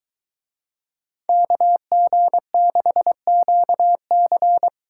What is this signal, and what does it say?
Morse code, heard as a keyed tone, about 23 wpm, 700 Hz. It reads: KG6QC